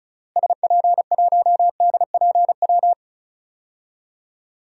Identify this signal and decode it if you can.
Morse code, heard as a keyed tone, about 36 wpm, 700 Hz. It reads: SP1DPW
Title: DXCC-100 Call Signs - 6 Characters - Repeated 2x - 35wpm